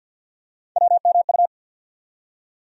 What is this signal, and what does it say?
Morse code, heard as a keyed tone, about 52 wpm, 700 Hz. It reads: WMU
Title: USA Call Sign Postfixes - 3 Letters 50wpm